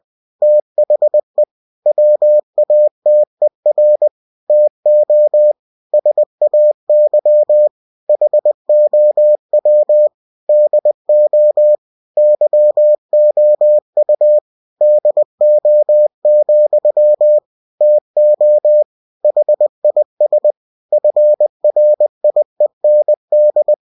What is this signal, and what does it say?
Morse code, heard as a keyed tone, about 20 wpm, 600 Hz. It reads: THE WATER TO SAY HOW DO YOU DO, TO HIS FRIEND
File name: cw-books / winnie-the-pooh / winnie-20wpm-page-0096